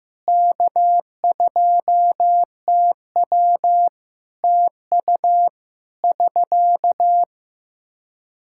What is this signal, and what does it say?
Morse code, heard as a keyed tone, about 15 wpm, 700 Hz. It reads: K2TW TU <SK>
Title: Practice QSO - Exchange 4 - Ending - 15wpm